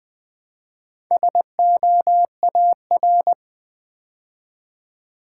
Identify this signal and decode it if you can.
Morse code, heard as a keyed tone, about 20 wpm, 700 Hz. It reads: SOAR
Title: Build A Word - 2 Letters At A Time - 4-Letter Words - 20wpm